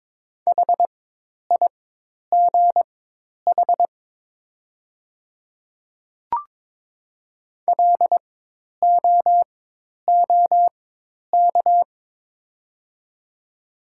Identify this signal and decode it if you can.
Morse code, heard as a keyed tone, about 22 wpm, spaced as for 10 wpm, 700 Hz. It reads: HIGH LOOK
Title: 4x Character Spacing - Top 500 Words - 22wpm